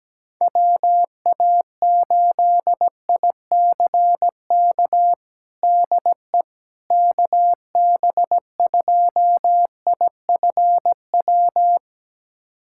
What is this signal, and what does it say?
Morse code, heard as a keyed tone, about 17 wpm, 700 Hz. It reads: WA8ICK DE KB2IFW
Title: USA Callsign DE Callsign 17wpm